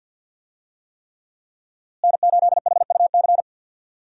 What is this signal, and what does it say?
Morse code, heard as a keyed tone, about 50 wpm, 700 Hz. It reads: N8HUC
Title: USA Call Signs - 5 Characters - Repeated 3x - 50wpm